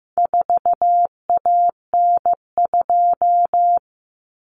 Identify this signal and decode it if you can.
Morse code, heard as a keyed tone, about 15 wpm, 700 Hz. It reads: 4AN2
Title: Common Cut Numbers - 4-Digit 15wpm